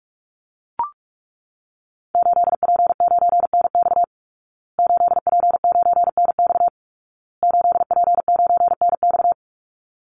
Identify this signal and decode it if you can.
Morse code, heard as a keyed tone, about 44 wpm, 700 Hz. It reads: 8P9NX 8P9NX 8P9NX
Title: Call Signs Repeated 2x - 45wpm